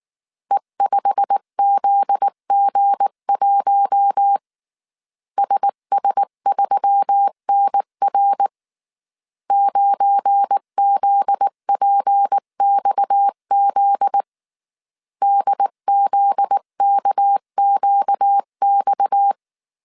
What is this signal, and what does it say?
Morse code, heard as a keyed tone, about 19 wpm, 800 Hz. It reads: E5ZG1 SS3NR 9ZPXZ DZKQX
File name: CW / DET / 1 9 1500C S C 18wpm 2539